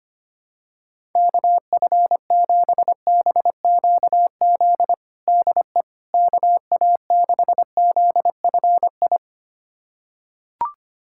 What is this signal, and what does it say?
Morse code, heard as a keyed tone, about 25 wpm, 700 Hz. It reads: KF7BQZ DE KA6ZFI
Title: USA Callsign DE Callsign - 6 Characters - 25wpm